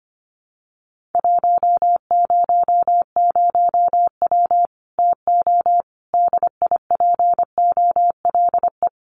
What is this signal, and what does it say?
Morse code, heard as a keyed tone, about 25 wpm, 700 Hz. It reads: 100W TO DIPOLE
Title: Common QSO Phrases 25wpm